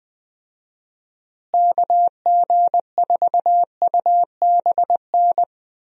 Morse code, heard as a keyed tone, about 20 wpm, 700 Hz. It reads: KG4UBN
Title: USA Call Signs - 6 Characters - Repeated 3x - 20wpm